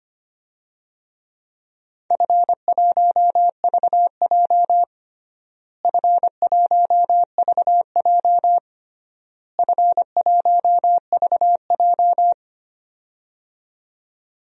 Morse code, heard as a keyed tone, about 25 wpm, 700 Hz. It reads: F1VJ F1VJ F1VJ
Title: DXCC-100 Call Signs - 4 Characters - Repeated 3x - 25wpm